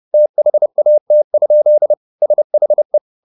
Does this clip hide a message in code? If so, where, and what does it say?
Yes: Morse code from 0.1 to 3.0 s, heard as a keyed tone, about 30 wpm, 600 Hz. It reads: THAT? SHE